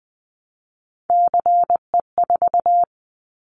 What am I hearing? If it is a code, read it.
Morse code, heard as a keyed tone, about 20 wpm, 700 Hz. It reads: CE4
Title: Callsign Prefixes for Top 100 DXCC - 20wpm